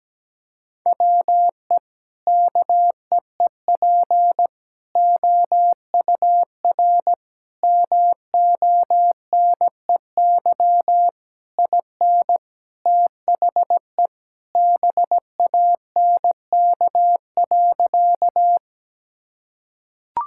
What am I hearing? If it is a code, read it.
Morse code, heard as a keyed tone, about 17 wpm, 700 Hz. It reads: WE KEEP OUR MONEY IN THE BANK.
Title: Sentences from Top 1000 Words - Review Part 6 - 17wpm